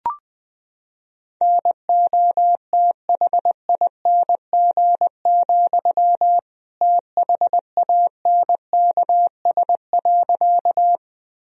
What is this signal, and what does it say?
Morse code, heard as a keyed tone, about 20 wpm, 700 Hz. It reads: NOTHING, THANKS.
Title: Sentences from Top 1000 Words - Review Part 3 - 20wpm